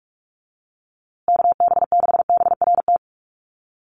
Morse code, heard as a keyed tone, about 44 wpm, 700 Hz. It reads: KB6BRT
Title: USA Call Signs Repeated 3x 45wpm